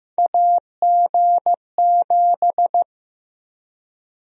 Morse code, heard as a keyed tone, about 15 wpm, 700 Hz. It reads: AG7
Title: USA Call Sign Prefixes - Repeated 2x - 15wpm